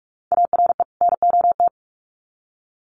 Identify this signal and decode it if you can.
Morse code, heard as a keyed tone, about 46 wpm, 700 Hz. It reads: ARE NOT
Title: Sets of 2 Words 45wpm